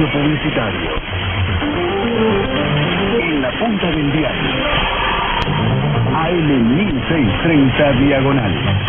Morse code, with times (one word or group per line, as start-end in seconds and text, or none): none